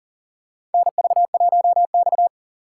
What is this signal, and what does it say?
Morse code, heard as a keyed tone, about 40 wpm, 700 Hz. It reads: NV1X
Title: USA Call Signs - 6x Character Spacing - 40wpm